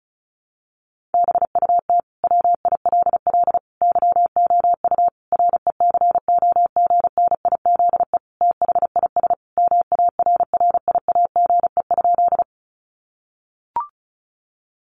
Morse code, heard as a keyed tone, about 34 wpm, 700 Hz. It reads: BUT WILL YOU RECOGNIZE THIS MARRIAGE?
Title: Sentences from Top 1400 Words 35wpm